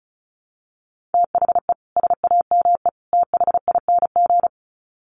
Morse code, heard as a keyed tone, about 34 wpm, 700 Hz. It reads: THE SAME THING
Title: Sets of 3 Words 35wpm